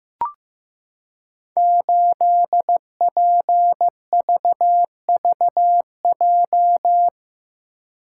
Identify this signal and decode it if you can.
Morse code, heard as a keyed tone, about 15 wpm, 700 Hz. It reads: 8PVVJ
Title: Code Groups - 5 Letter:Numbers - 15wpm